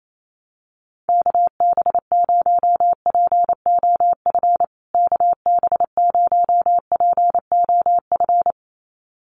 Morse code, heard as a keyed tone, about 28 wpm, 700 Hz. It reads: KB0POF KB0POF
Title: USA Call Signs Repeated 2x 28wpm